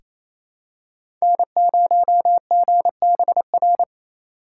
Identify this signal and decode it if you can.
Morse code, heard as a keyed tone, about 28 wpm, 700 Hz. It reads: N0GBR
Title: USA Call Signs - 5 Characters - 2x Character Spacing - 28wpm